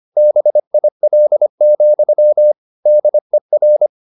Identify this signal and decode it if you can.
Morse code, heard as a keyed tone, about 25 wpm, 600 Hz. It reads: BIL, DER
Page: Ferie 25 wpm